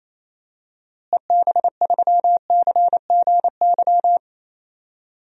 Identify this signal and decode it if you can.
Morse code, heard as a keyed tone, about 28 wpm, 700 Hz. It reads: EB3CGY
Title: DXCC-100 Call Signs - 6 Characters - Repeated 3x - 28wpm